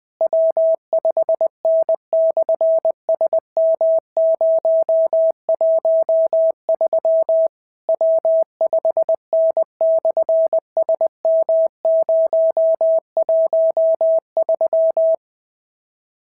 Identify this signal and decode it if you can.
Morse code, heard as a keyed tone, about 20 wpm, 650 Hz. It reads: W5N/SM013 W5N/SM013
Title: SOTA QSO Practice 20wpm